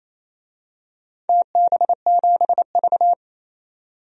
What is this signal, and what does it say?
Morse code, heard as a keyed tone, about 28 wpm, 700 Hz. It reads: TB7V
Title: DXCC-100 Call Signs - 4 Characters - 28wpm